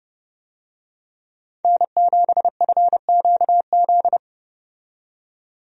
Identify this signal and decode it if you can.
Morse code, heard as a keyed tone, about 30 wpm, 700 Hz. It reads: N7FQZ
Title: USA Call Signs - 5 Characters - 30wpm